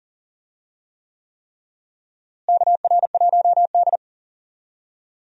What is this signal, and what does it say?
Morse code, heard as a keyed tone, about 40 wpm, 700 Hz. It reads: KR1D
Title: USA Call Signs - 4 Characters - Repeated 3x - 40wpm